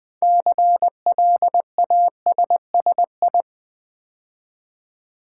Morse code, heard as a keyed tone, about 20 wpm, 700 Hz. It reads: CLASSI
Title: Build A Word - 3 Letters At A Time - 8-Letter Words - 20wpm